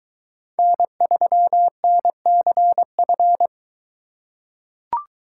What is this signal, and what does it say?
Morse code, heard as a keyed tone, about 22 wpm, 700 Hz. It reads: N3NCF